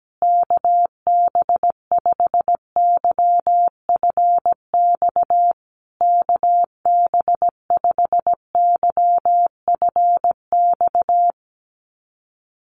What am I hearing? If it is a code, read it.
Morse code, heard as a keyed tone, about 17 wpm, 700 Hz. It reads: KB5YFX KB5YFX
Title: USA Call Signs Repeated 3x 17wpm